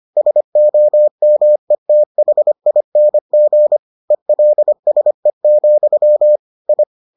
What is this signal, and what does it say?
Morse code, heard as a keyed tone, about 25 wpm, 600 Hz. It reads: SOMETHING ELSE, I